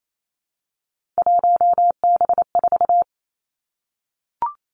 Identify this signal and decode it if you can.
Morse code, heard as a keyed tone, about 28 wpm, 690 Hz. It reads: 1B4
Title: Three Challenging Letter-Number Combinations 28wpm